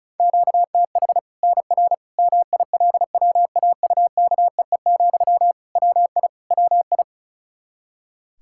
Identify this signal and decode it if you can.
Morse code, heard as a keyed tone, about 36 wpm, 700 Hz. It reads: QTH NR MILWAUKEE, WI WI
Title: K1USN SST QSO Practice 35wpm